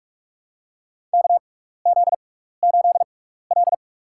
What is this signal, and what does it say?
Morse code, heard as a keyed tone, about 44 wpm, spaced as for 15 wpm, 700 Hz. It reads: KZ8L
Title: USA Call Signs - 4 Characters - 6x Character Spacing - 45wpm